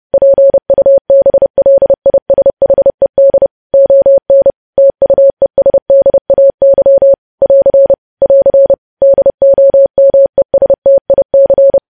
Morse code, heard as a keyed tone, about 30 wpm, 550 Hz. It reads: PUBLISHED ON TUESDAY <AR> <AR> DOMESTIC
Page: news-headlines-in-morse-code-at-30-wpm